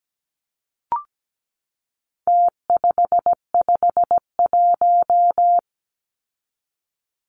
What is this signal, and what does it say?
Morse code, heard as a keyed tone, about 17 wpm, 700 Hz. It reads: T551
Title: Numbers - Contest Style - 4-Digit 17wpm